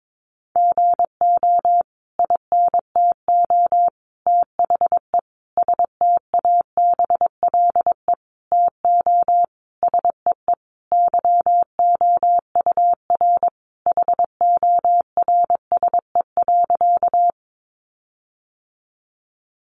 Morse code, heard as a keyed tone, about 22 wpm, 700 Hz. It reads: GO INTO THE STABLE TO SEE YOUR HORSE.
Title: Sentences from Top 1500 Words 22wpm